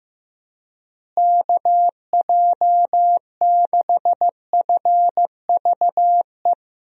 Morse code, heard as a keyed tone, about 15 wpm, 700 Hz. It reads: KJ6FVE
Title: USA Call Signs - 6 Characters - Repeated 3x - 15wpm